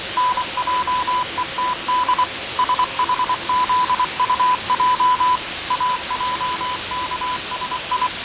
Morse code, heard as a keyed tone, about 24 wpm, 1000 Hz, with radio noise. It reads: NJETD SHZUJ AJKSI
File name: fav22-6824usb-20251029-1420z-voice-test-win10